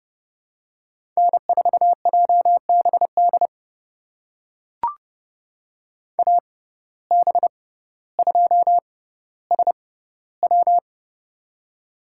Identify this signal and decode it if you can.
Morse code, heard as a keyed tone, about 30 wpm, spaced as for 15 wpm, 700 Hz. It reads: N4JBD AB2SW
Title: USA Call Signs - 5 Characters - 6x Character Spacing - 30wpm